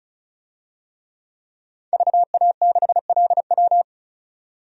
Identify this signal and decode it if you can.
Morse code, heard as a keyed tone, about 34 wpm, 700 Hz. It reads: VA6LW
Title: DXCC-100 Call Signs - 5 Characters - 35wpm